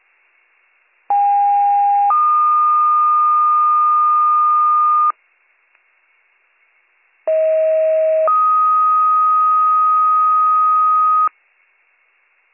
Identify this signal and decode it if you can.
Two-tone page, heard as a one-second tone, then a three-second tone, about 810 then 1220 Hz, about 640 then 1250 Hz. It is heard from 1.1 to 11.3 s.